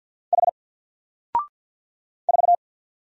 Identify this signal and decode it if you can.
Morse code, heard as a keyed tone, about 50 wpm, 700 Hz. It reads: H 4
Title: Single Character (Letter:Number:Punctuation:Prosign) 50wpm